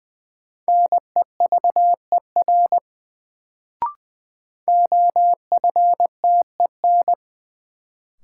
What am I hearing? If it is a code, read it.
Morse code, heard as a keyed tone, about 20 wpm, 700 Hz. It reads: NEVER OFTEN
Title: Sentences from Top 300 Words - New Word Drill - 20wpm